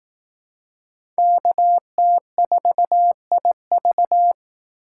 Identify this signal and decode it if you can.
Morse code, heard as a keyed tone, about 19 wpm, 700 Hz. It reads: KT4IV
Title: Speed Racing - USA 2x2 Call Signs 15wpm